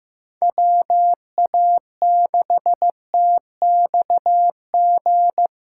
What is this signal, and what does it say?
Morse code, heard as a keyed tone, about 18 wpm, 700 Hz. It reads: WA6TXG
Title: Speed Racing - USA 2x3 Call Signs 15wpm